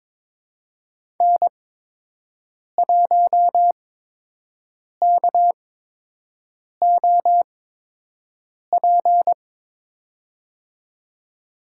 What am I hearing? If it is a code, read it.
Morse code, heard as a keyed tone, about 22 wpm, spaced as for 6 wpm, 700 Hz. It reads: N1KOP
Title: USA Call Signs - 8x Character Spacing - 22wpm